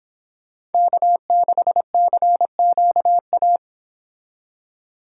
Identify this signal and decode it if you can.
Morse code, heard as a keyed tone, about 26 wpm, 700 Hz. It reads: K6CQA